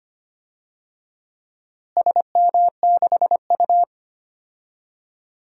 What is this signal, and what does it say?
Morse code, heard as a keyed tone, about 25 wpm, 700 Hz. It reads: SM6U
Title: DXCC-100 Call Signs - 4 Characters - Repeated 2x - 25wpm